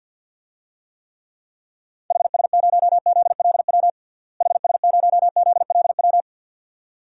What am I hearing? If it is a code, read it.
Morse code, heard as a keyed tone, about 52 wpm, 700 Hz. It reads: HS0ZLW HS0ZLW
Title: DXCC-100 Call Signs - 6 Characters - Repeated 2x - 50wpm